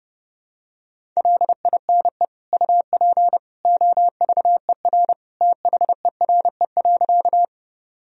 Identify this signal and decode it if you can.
Morse code, heard as a keyed tone, about 30 wpm, 700 Hz. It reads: LINE UP OVER THERE.
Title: Sentences from Top 1000 Words - Review Part 1 - 30wpm